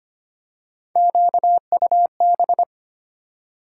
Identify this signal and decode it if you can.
Morse code, heard as a keyed tone, about 31 wpm, 700 Hz. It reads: QUB
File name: Speed-Racing - Three Letter-Number Combinations - 25wpm